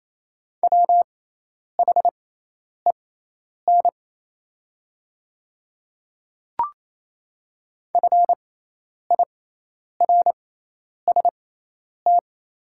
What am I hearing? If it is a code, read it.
Morse code, heard as a keyed tone, about 28 wpm, spaced as for 10 wpm, 690 Hz. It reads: WHEN FIRST
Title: 6x Character Spacing - Top 300 Words - 28wpm